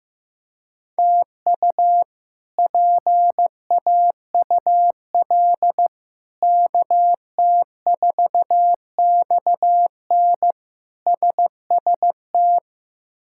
Morse code, heard as a keyed tone, about 16 wpm, 700 Hz. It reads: TU PAUL KT4XN SST
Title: K1USN SST QSO Practice - 15wpm